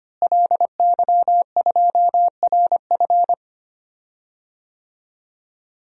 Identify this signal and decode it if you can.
Morse code, heard as a keyed tone, about 25 wpm, 700 Hz. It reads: LY2RF
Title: DXCC-100 Call Signs - 5 Characters - Repeated 2x - 25wpm